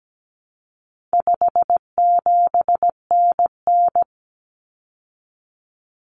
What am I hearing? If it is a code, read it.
Morse code, heard as a keyed tone, about 17 wpm, 700 Hz. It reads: 57NN
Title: Common Cut Numbers - 4-Digit 17wpm